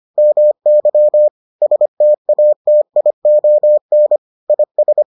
Morse code, heard as a keyed tone, about 25 wpm, 600 Hz. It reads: MY STATION IS